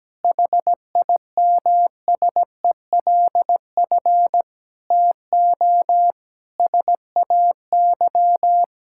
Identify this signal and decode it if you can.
Morse code, heard as a keyed tone, about 17 wpm, 700 Hz. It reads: HIMSELF TO SAY
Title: Sentences from Top 1600 Words 17wpm